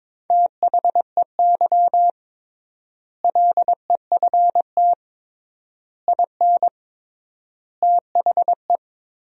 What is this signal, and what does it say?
Morse code, heard as a keyed tone, about 22 wpm, 700 Hz. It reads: THEY LEFT IN THE
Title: 3x Word-Spacing - Sentences from Top 400 Words - 22wpm